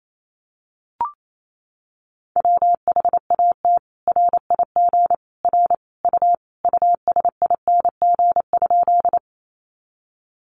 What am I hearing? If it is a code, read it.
Morse code, heard as a keyed tone, about 28 wpm, 700 Hz. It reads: WHAT RIG R U USING?
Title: Common QSO Phrases 28wpm